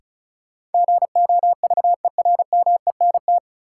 Morse code, heard as a keyed tone, about 36 wpm, 700 Hz. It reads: GOVERMENT
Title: Sentences from Top 1300 Words - New Word Drill - 35wpm